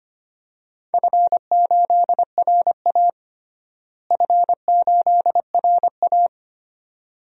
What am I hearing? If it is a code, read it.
Morse code, heard as a keyed tone, about 25 wpm, 700 Hz. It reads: F8RA F8RA
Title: DXCC-100 Call Signs - 4 Characters - Repeated 2x - 25wpm